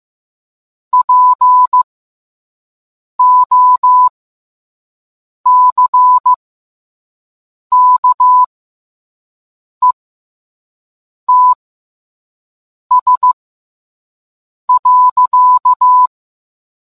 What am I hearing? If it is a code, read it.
Morse code, heard as a keyed tone, about 15 wpm, spaced as for 5 wpm, 1000 Hz. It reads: POCKETS.